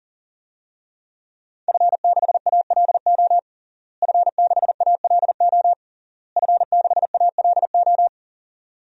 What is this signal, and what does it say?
Morse code, heard as a keyed tone, about 37 wpm, 700 Hz. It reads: F6ALO F6ALO F6ALO
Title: DXCC-100 Call Signs - 5 Characters - Repeated 3x - 40wpm